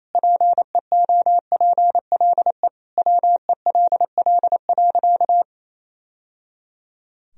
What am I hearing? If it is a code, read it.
Morse code, heard as a keyed tone, about 28 wpm, 700 Hz. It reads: PEOPLE WELL.
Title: Sentences from Top 1000 Words - Review Part 2 - 28wpm